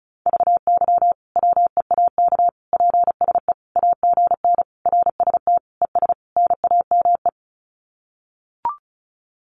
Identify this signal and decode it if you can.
Morse code, heard as a keyed tone, about 34 wpm, 700 Hz. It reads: VY WEAK PSE AGN RST ES NAME
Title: Common QSO Phrases 35wpm